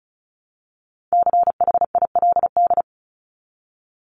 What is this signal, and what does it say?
Morse code, heard as a keyed tone, about 34 wpm, 700 Hz. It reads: CHILD
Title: Top 501-750 Words 35wpm